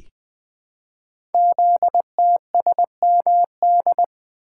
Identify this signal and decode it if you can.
Morse code, heard as a keyed tone, about 20 wpm, 700 Hz. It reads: ZTSMD
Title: Code Groups - 5 Letters - 20wpm